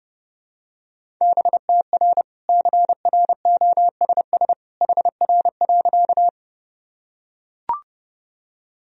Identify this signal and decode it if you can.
Morse code, heard as a keyed tone, about 30 wpm, 700 Hz. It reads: BTR CROSS HR.
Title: Sentences from Top 400 Words and Abbreviations 30wpm